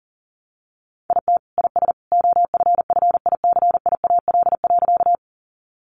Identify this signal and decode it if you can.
Morse code, heard as a keyed tone, about 40 wpm, 700 Hz. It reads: IT IS OFFICIAL.
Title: Sentences from Top 1000 Words 40wpm